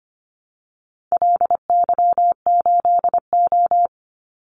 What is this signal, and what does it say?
Morse code, heard as a keyed tone, about 25 wpm, 700 Hz. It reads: LY8O
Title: Call Signs Repeated 2x - 25wpm